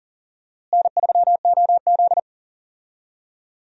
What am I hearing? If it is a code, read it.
Morse code, heard as a keyed tone, about 40 wpm, 700 Hz. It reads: N3OZ
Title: USA Call Signs - 4 Characters - Repeated 2x - 40wpm